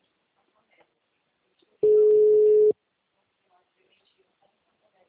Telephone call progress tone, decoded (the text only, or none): ringback tone